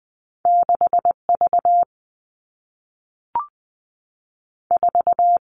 Morse code, heard as a keyed tone, about 20 wpm, 700 Hz. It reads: 6 V 4
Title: All Cut Numbers - 2-Digit 20wpm